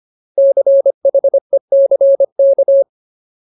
Morse code, heard as a keyed tone, about 25 wpm, 550 Hz. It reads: CHECK
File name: Ultimate Word List / 25WPM Ultimate